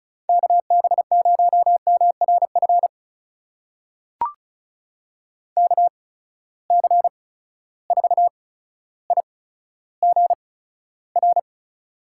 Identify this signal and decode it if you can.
Morse code, heard as a keyed tone, about 36 wpm, spaced as for 15 wpm, 700 Hz. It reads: KB0MRF KC4IGR
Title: USA Call Signs - 6 Characters - 8x Character Spacing - 35wpm